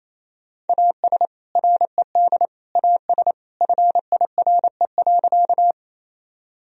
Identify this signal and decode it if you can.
Morse code, heard as a keyed tone, about 28 wpm, 700 Hz. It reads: AS RED AS FIRE.
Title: Sentences from Top 1000 Words - Review Part 2 - 28wpm